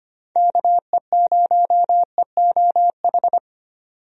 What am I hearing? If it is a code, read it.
Morse code, heard as a keyed tone, about 25 wpm, 700 Hz. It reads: KE0EOH
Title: USA Call Signs - 6 Characters - 25wpm